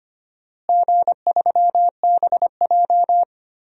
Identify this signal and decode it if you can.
Morse code, heard as a keyed tone, about 25 wpm, 700 Hz. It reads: G3BJ
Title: DXCC-100 Call Signs - 4 Characters - Repeated 2x - 25wpm